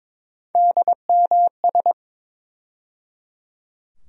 Morse code, heard as a keyed tone, about 22 wpm, 700 Hz. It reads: DMS
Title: USA Call Sign Postfixes - 3 Letters 22wpm